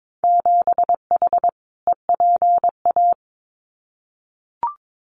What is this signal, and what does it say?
Morse code, heard as a keyed tone, about 22 wpm, 700 Hz. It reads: 7H EPA
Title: Winter Field Day Exchanges Encore 22wpm